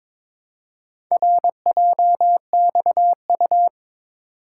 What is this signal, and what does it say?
Morse code, heard as a keyed tone, about 22 wpm, 700 Hz. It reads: RJXU
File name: practice / 22wpm / Four Letter-Number Combinations - 22wpm